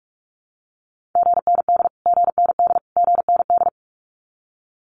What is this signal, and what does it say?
Morse code, heard as a keyed tone, about 44 wpm, 700 Hz. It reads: GND GND GND
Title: Abbreviations Repeated 3x - 45wpm